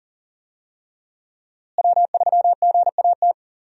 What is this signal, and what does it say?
Morse code, heard as a keyed tone, about 40 wpm, 700 Hz. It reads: W3GAT
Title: USA Call Signs - 5 Characters - Repeated 3x - 40wpm